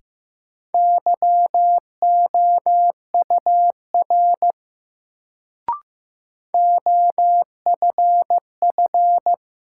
Morse code, heard as a keyed tone, about 15 wpm, 700 Hz. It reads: YOUR OFF
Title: Sets of 2 Words - New Word Drill - 15wpm